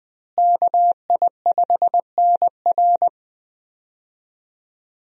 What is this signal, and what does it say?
Morse code, heard as a keyed tone, about 20 wpm, 700 Hz. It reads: KI5NR
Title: USA Call Signs - 5 Characters - Repeated 2x - 20wpm